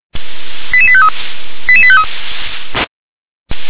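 Five-tone selective call (ZVEI), heard as quick runs of five tones, tones of about 70 ms. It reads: 88853 88853
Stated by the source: ZVEI five-tone selective call received with an SDR.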